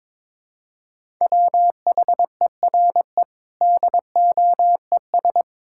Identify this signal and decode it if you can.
Morse code, heard as a keyed tone, about 22 wpm, 700 Hz. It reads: WHERE DOES
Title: Sentences from Top 1000 Words - Review Part 6 - 22wpm